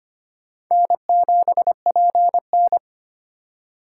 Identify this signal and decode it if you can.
Morse code, heard as a keyed tone, about 25 wpm, 700 Hz. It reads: N7PN
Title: USA Call Signs - 4 Characters - 4x Character Spacing - 25wpm